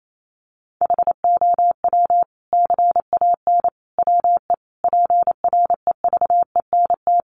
Morse code, heard as a keyed tone, about 28 wpm, 700 Hz. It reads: HOW CAN WE PREVENT
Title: Sentences from Top 1100 Words 28wpm